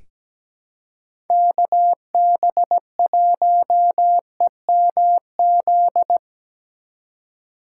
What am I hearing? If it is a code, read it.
Morse code, heard as a keyed tone, about 17 wpm, 700 Hz. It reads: KB1EMZ
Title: USA Call Signs - 6 Characters - 4x Character Spacing - 17wpm